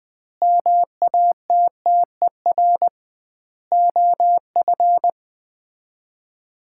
Morse code, heard as a keyed tone, about 20 wpm, 700 Hz. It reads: MATTER OF
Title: 2x Word-Spacing - Sets of 3 Words - 20wpm